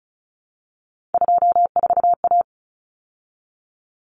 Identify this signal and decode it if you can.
Morse code, heard as a keyed tone, about 34 wpm, 700 Hz. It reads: 24A
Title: Common Cut Numbers - 3-Digit 35wpm